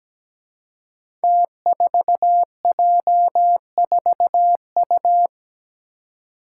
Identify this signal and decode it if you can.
Morse code, heard as a keyed tone, about 17 wpm, 700 Hz. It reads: T4J4U
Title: Five Challenging Letter-Number Combinations 17wpm